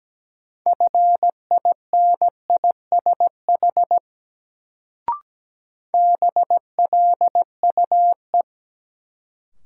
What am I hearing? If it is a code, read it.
Morse code, heard as a keyed tone, about 17 wpm, 700 Hz. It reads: FINISH BLUE
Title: Sentences from Top 1000 Words - Review Part 3 - New Word Drill - 17wpm